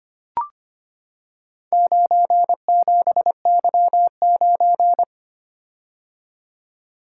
Morse code, heard as a keyed tone, about 25 wpm, 700 Hz. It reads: 97Y9
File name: practice / 25wpm / Four Letter-Number Combinations - 25wpm